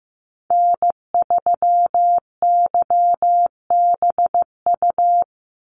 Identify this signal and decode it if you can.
Morse code, heard as a keyed tone, about 15 wpm, 700 Hz. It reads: N3YBU
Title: USA Call Signs 15wpm